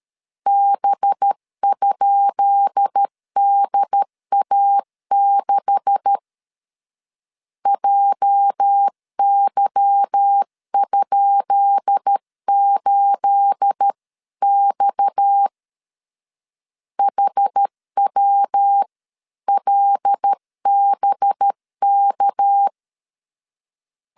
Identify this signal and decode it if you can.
Morse code, heard as a keyed tone, about 13 wpm, 800 Hz. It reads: B?DA6 JY?8X HWLBK
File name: LEZIONI / 10 SIGNS / 10B INT 12wpm 2414